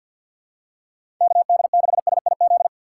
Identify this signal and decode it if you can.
Morse code, heard as a keyed tone, about 47 wpm, 700 Hz. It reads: KD6SIZ
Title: USA Call Signs - 6 Characters - 6x Character Spacing - 50wpm